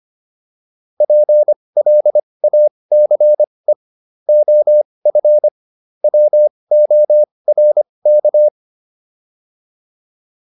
Morse code, heard as a keyed tone, about 25 wpm, spaced as for 20 wpm, 600 Hz. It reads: PLACE OF WORK